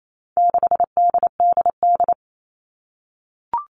Morse code, heard as a keyed tone, about 28 wpm, 700 Hz. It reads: 6DDD
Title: All Cut Numbers - 4-Digit 28wpm